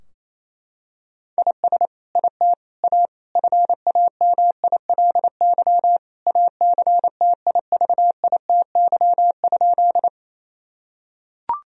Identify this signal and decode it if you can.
Morse code, heard as a keyed tone, about 28 wpm, 700 Hz. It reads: IS IT A FAMILY ACTIVITY?
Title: Sentences from Top 1000 Words - Review Part 4 - 28wpm